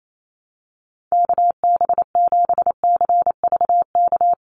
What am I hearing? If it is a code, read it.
Morse code, heard as a keyed tone, about 28 wpm, 700 Hz. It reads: KB7CVK
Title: USA Call Signs 28wpm